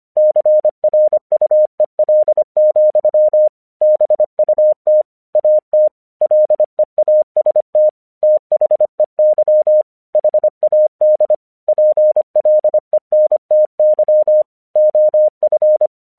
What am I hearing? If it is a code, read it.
Morse code, heard as a keyed tone, about 25 wpm, 610 Hz. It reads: CRUEL, BUT AT LEAST THEY HAD PLENTY OF
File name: 1 English 25wpm / AQC05